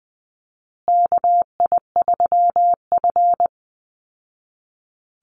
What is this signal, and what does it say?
Morse code, heard as a keyed tone, about 20 wpm, 700 Hz. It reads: KI3F
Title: Call Signs Repeated 2x - 20wpm